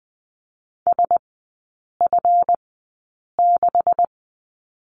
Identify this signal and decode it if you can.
Morse code, heard as a keyed tone, about 20 wpm, spaced as for 8 wpm, 700 Hz. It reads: SF6
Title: 184 Lesson 30e - Letter b - Three Characters 20wpm